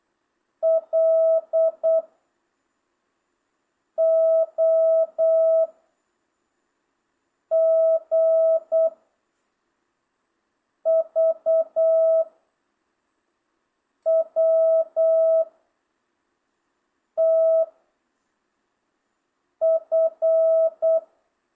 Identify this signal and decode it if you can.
Morse code, heard as a keyed tone, about 8 wpm, 650 Hz. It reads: L O G V W T F